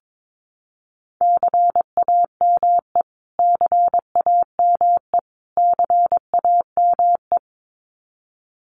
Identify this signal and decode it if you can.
Morse code, heard as a keyed tone, about 22 wpm, 700 Hz. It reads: CAME CAME CAME
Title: Top 300 Words Repeated 22wpm